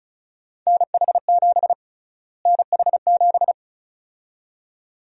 Morse code, heard as a keyed tone, about 34 wpm, 700 Hz. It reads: NH7 NH7
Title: USA Call Sign Prefixes - Repeated 2x - 35wpm